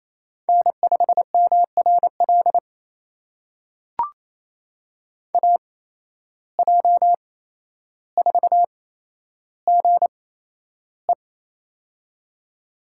Morse code, heard as a keyed tone, about 28 wpm, spaced as for 12 wpm, 700 Hz. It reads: N5MRL AJ4GE
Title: USA Call Signs - 5 Characters - 8x Character Spacing - 28wpm